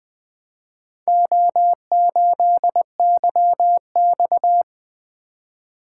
Morse code, heard as a keyed tone, about 20 wpm, 700 Hz. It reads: O8YX
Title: Four Letter-Number Combinations - 20wpm